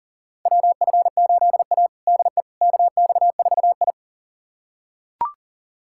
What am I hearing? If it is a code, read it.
Morse code, heard as a keyed tone, about 40 wpm, 700 Hz. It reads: WF8A DE KX4I
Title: USA Callsign DE Callsign - 4 Characters - 40wpm